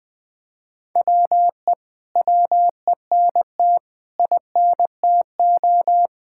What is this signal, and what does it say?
Morse code, heard as a keyed tone, about 20 wpm, 700 Hz. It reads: WE WENT INTO
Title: Sentences from Top 1600 Words 20wpm